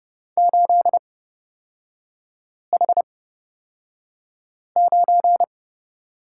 Morse code, heard as a keyed tone, about 30 wpm, spaced as for 5 wpm, 700 Hz. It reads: 8H9
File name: Lesson 36e - BT - Three Characters 30wpm